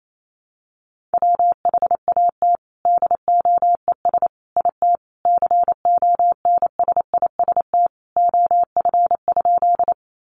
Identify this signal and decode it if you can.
Morse code, heard as a keyed tone, about 28 wpm, 700 Hz. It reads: WHAT DOES IT CONSIST OF?
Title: Sentences from Top 1400 Words 28wpm